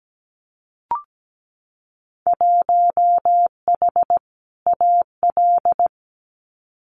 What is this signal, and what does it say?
Morse code, heard as a keyed tone, about 17 wpm, 700 Hz. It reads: 1H AL
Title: Winter Field Day Exchanges 17wpm